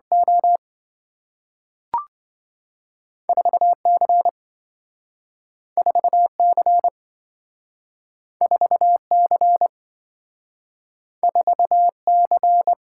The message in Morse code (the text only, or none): O 4C 4C 4C 4C